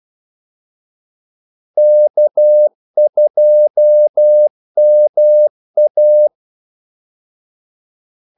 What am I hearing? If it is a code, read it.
Morse code, heard as a keyed tone, about 12 wpm, 600 Hz. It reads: K2MA
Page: 2 ekstra ordmellemrum mellem kaldesignalerne